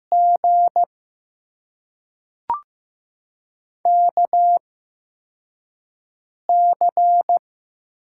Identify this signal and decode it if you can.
Morse code, heard as a keyed tone, about 15 wpm, spaced as for 3 wpm, 700 Hz. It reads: GKC
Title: USA Call Signs - 6 Characters - 8x Character Spacing - 15wpm